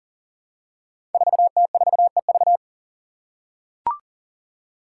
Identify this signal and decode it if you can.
Morse code, heard as a keyed tone, about 37 wpm, 690 Hz. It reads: 4T4EV
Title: Five Challenging Letter-Number Combinations 40wpm